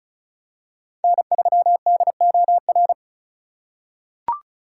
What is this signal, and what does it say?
Morse code, heard as a keyed tone, about 34 wpm, 700 Hz. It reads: N3DOR
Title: USA Call Signs - 5 Characters - 3x Character Spacing - 35wpm